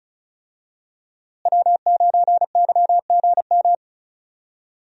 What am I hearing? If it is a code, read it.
Morse code, heard as a keyed tone, about 36 wpm, 700 Hz. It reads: W9YGM
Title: USA Call Signs - 5 Characters - Repeated 3x - 35wpm